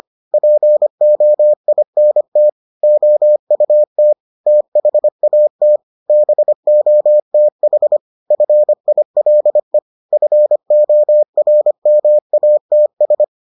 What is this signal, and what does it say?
Morse code, heard as a keyed tone, about 25 wpm, 600 Hz. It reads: POINT OUT THAT BOTH FILE FORMATS